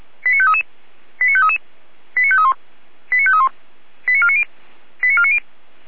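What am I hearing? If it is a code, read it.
Five-tone selective call (ZVEI), heard as quick runs of five tones, tones of about 70 ms. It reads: 89533 89533 89531 89531 89490 89490
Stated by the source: Five-tone ZVEI selective call 89518 received by SDR.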